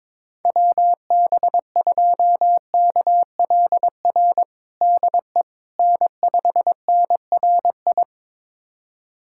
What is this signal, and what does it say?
Morse code, heard as a keyed tone, about 22 wpm, 700 Hz. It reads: WB2KLR DE N5NRI
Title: USA Callsign DE Callsign 22wpm